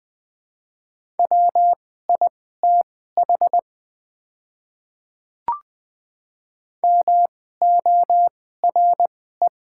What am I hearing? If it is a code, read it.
Morse code, heard as a keyed tone, about 20 wpm, spaced as for 14 wpm, 700 Hz. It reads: WITH MORE
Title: 2x Character Spacing - Top 100 Words - 20wpm